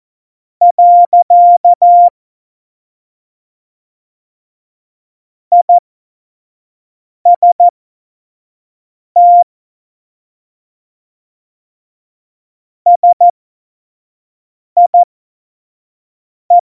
Morse code, heard as a keyed tone, about 14 wpm, spaced as for 5 wpm, 700 Hz. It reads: . IST SIE
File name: Morsekurs / Witze / Witz 22. Mai 2018, 25 F70